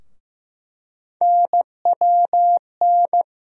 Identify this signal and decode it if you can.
Morse code, heard as a keyed tone, about 15 wpm, 700 Hz. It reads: NWN
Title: USA Call Sign Postfixes - 3 Letters 15wpm